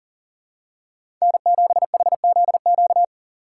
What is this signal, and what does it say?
Morse code, heard as a keyed tone, about 40 wpm, 700 Hz. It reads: N7HZQ
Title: USA Call Signs - 4x Character Spacing - 40wpm